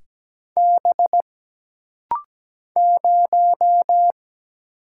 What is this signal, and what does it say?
Morse code, heard as a keyed tone, about 17 wpm, 700 Hz. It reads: B 0
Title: Single Letter-Number - Rapid-Fire 17wpm